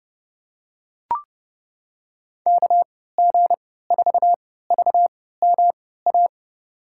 Morse code, heard as a keyed tone, about 32 wpm, spaced as for 17 wpm, 700 Hz. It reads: KG4VMA
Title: USA Call Signs - 6 Characters - 3x Character Spacing - 30wpm